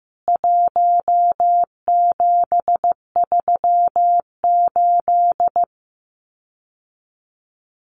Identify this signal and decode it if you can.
Morse code, heard as a keyed tone, about 15 wpm, 700 Hz. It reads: 1738
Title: CWT Exchange 15wpm